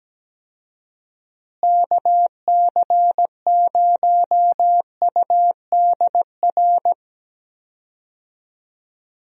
Morse code, heard as a keyed tone, about 17 wpm, 700 Hz. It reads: KC0UDR
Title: USA Call Signs - 6 Characters - Repeated 2x - 17wpm